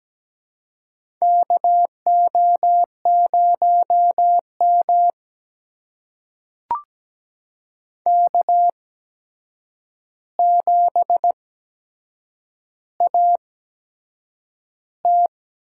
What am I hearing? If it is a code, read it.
Morse code, heard as a keyed tone, about 17 wpm, spaced as for 7 wpm, 700 Hz. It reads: KO0M K7AT
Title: USA Call Signs - 4 Characters - 8x Character Spacing - 17wpm